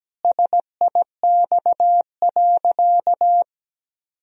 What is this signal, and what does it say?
Morse code, heard as a keyed tone, about 17 wpm, 700 Hz. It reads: SIX.
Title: Sentences from Top 1000 Words - Review Part 2 - 17wpm